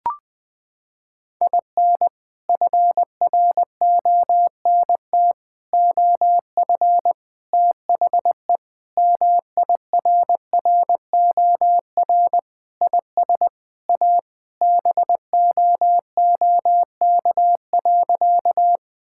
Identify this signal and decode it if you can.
Morse code, heard as a keyed tone, about 20 wpm, 700 Hz. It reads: IN FRONT OF THE MIRROR IS A BOOK.
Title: Sentences with 4-Word Phrases - 20wpm